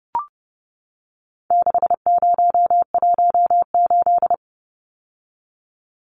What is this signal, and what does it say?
Morse code, heard as a keyed tone, about 30 wpm, 700 Hz. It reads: 6018
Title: Numbers - 4-Digit 30wpm